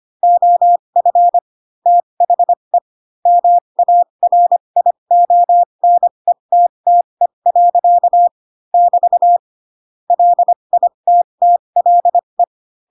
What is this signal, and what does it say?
Morse code, heard as a keyed tone, about 25 wpm, spaced as for 22 wpm, 700 Hz. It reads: OF THE MARIONETTE. <BT> LITTLE
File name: Pinocchio / Pinocchio ch3 18 25 ik1hge